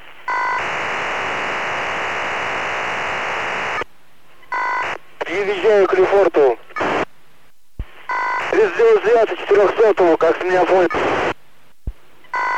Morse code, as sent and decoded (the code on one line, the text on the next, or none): none